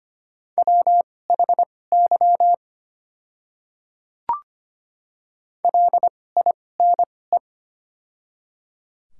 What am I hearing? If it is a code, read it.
Morse code, heard as a keyed tone, about 25 wpm, spaced as for 18 wpm, 700 Hz. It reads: WHY LINE
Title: 2x Character Spacing - Top 300 Words - 25wpm